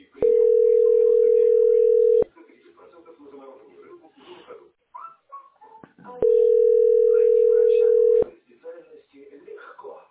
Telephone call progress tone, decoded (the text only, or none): ringback tone